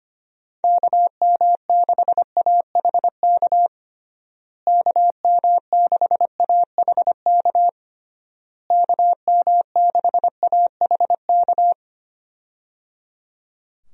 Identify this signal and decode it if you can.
Morse code, heard as a keyed tone, about 25 wpm, 700 Hz. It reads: KM6AHK KM6AHK KM6AHK
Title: USA Call Signs - 6 Characters - Repeated 3x - 25wpm